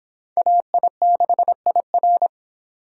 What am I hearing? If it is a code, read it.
Morse code, heard as a keyed tone, about 25 wpm, 700 Hz. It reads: AI6IR